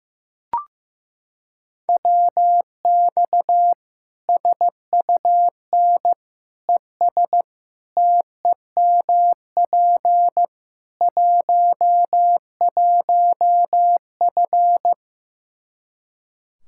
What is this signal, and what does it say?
Morse code, heard as a keyed tone, about 15 wpm, 700 Hz. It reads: WX SUN ES TEMP 11F
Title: Practice QSO - Exchange 2 - Info - 15wpm